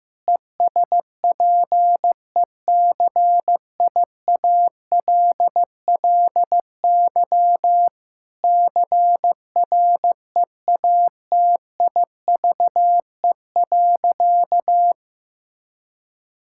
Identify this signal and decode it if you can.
Morse code, heard as a keyed tone, about 15 wpm, 700 Hz. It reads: ESPECIALLY CREATIVE.
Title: Sentences from Top 1000 Words - Review Part 5 - 15wpm